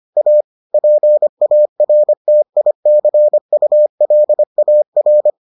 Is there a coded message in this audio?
Yes: Morse code from 0.2 to 5.3 s, heard as a keyed tone, about 25 wpm, 600 Hz. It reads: A PARTICULAR